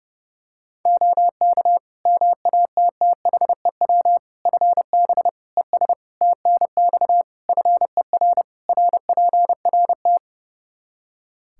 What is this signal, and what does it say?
Morse code, heard as a keyed tone, about 32 wpm, 700 Hz. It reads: OK MATTHEW FB ES TNX FER RPRT
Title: Practice QSO - Exchange 2 - Info - 30wpm